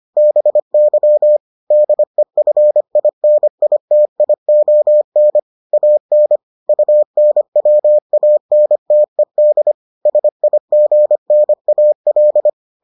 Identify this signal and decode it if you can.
Morse code, heard as a keyed tone, about 25 wpm, 600 Hz. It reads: BY DEFINITION AN UNWANTED SIGNAL